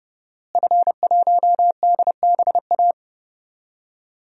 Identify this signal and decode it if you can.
Morse code, heard as a keyed tone, about 30 wpm, 700 Hz. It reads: F1DBA
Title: DXCC-100 Call Signs - 5 Characters - Repeated 3x - 30wpm